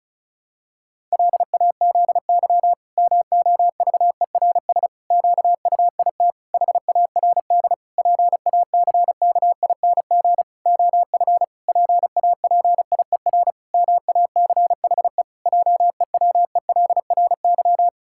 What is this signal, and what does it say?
Morse code, heard as a keyed tone, about 36 wpm, 700 Hz. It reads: LAZY MOVERS QUIT HARD PACKING OF PAPIER MACHE JEWELRY
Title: Pangram Challenge v2 - 35wpm